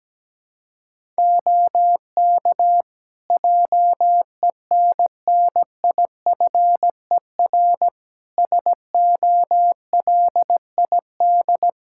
Morse code, heard as a keyed tone, about 17 wpm, 700 Hz. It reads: OK JENNIFER SOLID
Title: Practice QSO - Exchange 3 - More Info - 17wpm